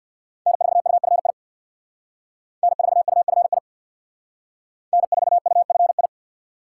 Morse code, heard as a keyed tone, about 59 wpm, 700 Hz. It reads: N4UFI N4UFI N4UFI
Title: Speed Racing - USA 1x3 Call Signs 50wpm